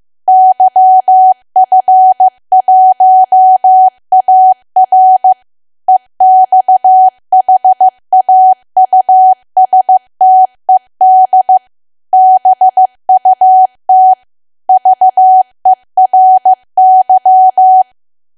Morse code, heard as a keyed tone, about 15 wpm, 750 Hz. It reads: YF1AR EXHAUSTED BUT VERY
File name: cwa / W1AW-Level2 / 15WPM